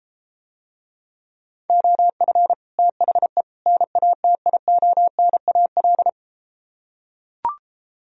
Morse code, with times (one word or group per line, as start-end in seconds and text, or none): 1.7-2.5 OF
2.8-3.4 THE
3.7-6.1 NATIONAL